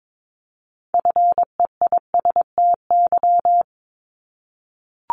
Morse code, heard as a keyed tone, about 22 wpm, 700 Hz. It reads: FEISTY
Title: Multi-letter Phonemes 22wpm